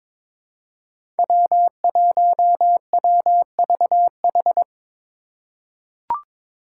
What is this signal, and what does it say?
Morse code, heard as a keyed tone, about 22 wpm, 700 Hz. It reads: W1WVH
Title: Five Challenging Letter-Number Combinations 22wpm